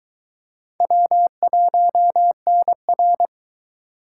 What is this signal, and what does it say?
Morse code, heard as a keyed tone, about 22 wpm, 700 Hz. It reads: W1NR